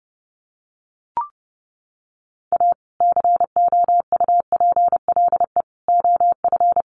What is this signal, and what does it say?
Morse code, heard as a keyed tone, about 30 wpm, 700 Hz. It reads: A COUPLE OF
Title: Sets of 4 Words 30wpm